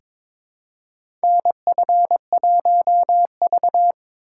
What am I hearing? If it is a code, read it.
Morse code, heard as a keyed tone, about 22 wpm, 700 Hz. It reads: NF1V
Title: USA Call Signs - 4 Characters - 2x Character Spacing - 22wpm